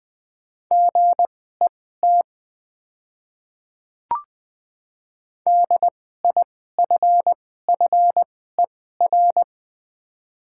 Morse code, heard as a keyed tone, about 20 wpm, spaced as for 14 wpm, 700 Hz. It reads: GET DIFFER
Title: 2x Character Spacing - Top 200 Words - 20wpm